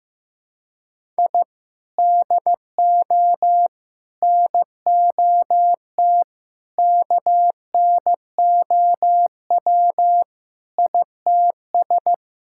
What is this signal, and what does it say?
Morse code, heard as a keyed tone, about 15 wpm, 700 Hz. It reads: I DO NOT KNOW ITS
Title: Sentences from Top 1000 Words - Review Part 6 - 15wpm